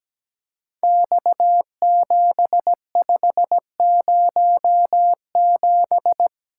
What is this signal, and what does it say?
Morse code, heard as a keyed tone, about 17 wpm, 700 Hz. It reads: X7507
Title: Code Groups - 5 Letter:Numbers - 17wpm